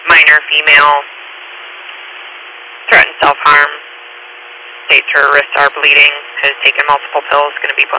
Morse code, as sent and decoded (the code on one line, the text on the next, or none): none